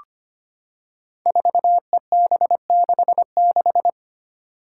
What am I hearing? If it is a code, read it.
Morse code, heard as a keyed tone, about 25 wpm, 700 Hz. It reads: 4EB66
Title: Five Challenging Letter-Number Combinations 25wpm